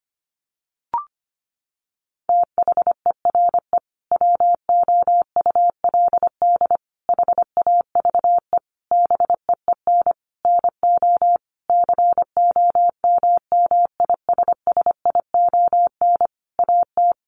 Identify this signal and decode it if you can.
Morse code, heard as a keyed tone, about 25 wpm, 700 Hz. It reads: THERE WOULD HAVE BEEN NO COMMISSION AT
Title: Sentences from Top 1400 Words 25wpm